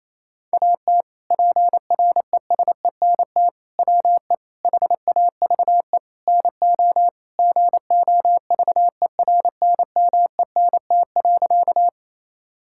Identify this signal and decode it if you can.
Morse code, heard as a keyed tone, about 27 wpm, 700 Hz. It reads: AT PRESENT WE HAVE NO GOVERNMENT.
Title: Sentences from Top 1000 Words - Review Part 5 - 28wpm